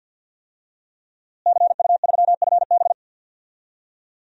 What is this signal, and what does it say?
Morse code, heard as a keyed tone, about 48 wpm, 700 Hz. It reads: CU3FB
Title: DXCC-100 Call Signs - 5 Characters - Repeated 2x - 50wpm